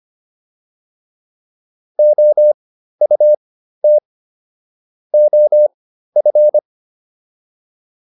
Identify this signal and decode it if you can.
Morse code, heard as a keyed tone, about 25 wpm, spaced as for 13 wpm, 600 Hz. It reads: OUT OF